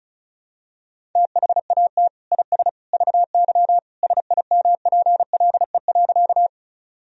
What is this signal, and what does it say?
Morse code, heard as a keyed tone, about 34 wpm, 700 Hz. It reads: THAT IS VY SIMPLE.
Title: Sentences from Top 400 Words and Abbreviations 35wpm